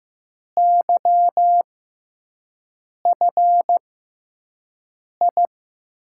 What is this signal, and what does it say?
Morse code, heard as a keyed tone, about 15 wpm, spaced as for 5 wpm, 700 Hz. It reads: YFI
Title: USA Call Signs - 6 Characters - 6x Character Spacing - 15wpm